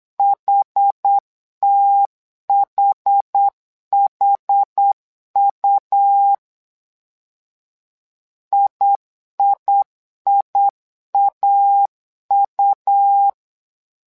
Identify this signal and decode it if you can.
Morse code, heard as a keyed tone, about 8 wpm, 800 Hz. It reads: HTHHU IIIAU
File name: LEZIONI / 2 UHDN / 2 UH 8wpm 2129